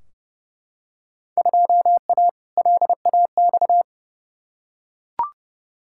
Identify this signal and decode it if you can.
Morse code, heard as a keyed tone, about 30 wpm, 700 Hz. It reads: 2A LAX
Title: Field Day Exchanges - 30wpm